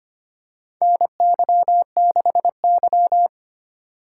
Morse code, heard as a keyed tone, about 25 wpm, 700 Hz. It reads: NY6Y
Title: K1USN SST QSO Practice - 25wpm